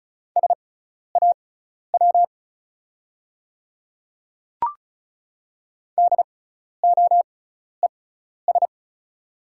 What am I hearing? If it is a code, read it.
Morse code, heard as a keyed tone, about 36 wpm, spaced as for 12 wpm, 690 Hz. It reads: SAW DOES
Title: 6x Character Spacing - Top 300 Words - 35wpm